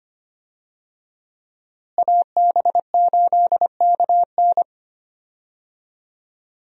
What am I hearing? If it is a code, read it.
Morse code, heard as a keyed tone, about 25 wpm, 700 Hz. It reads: AB8KN
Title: USA Call Signs - 5 Characters - Repeated 2x - 25wpm